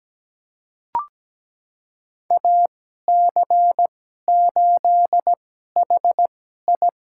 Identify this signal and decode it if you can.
Morse code, heard as a keyed tone, about 17 wpm, spaced as for 12 wpm, 700 Hz. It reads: AC8HI
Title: USA Call Signs - 2x Character Spacing - 17wpm